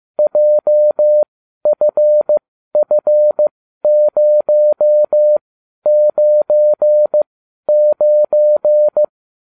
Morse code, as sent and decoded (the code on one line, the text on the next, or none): .--- ..-. ..-. ----- ----. ----.
JFF099